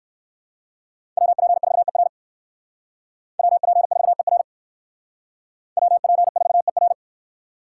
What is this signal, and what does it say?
Morse code, heard as a keyed tone, about 65 wpm, 700 Hz. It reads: WP4ER WP4ER WP4ER